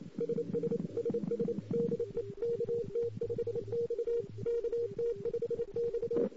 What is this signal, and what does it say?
Morse code, heard as a keyed tone, about 28 wpm, 470 Hz. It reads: SS SS DE KT5X KT5X
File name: CW / 2008-ss-cw-k5tr-op-n5rz -04:00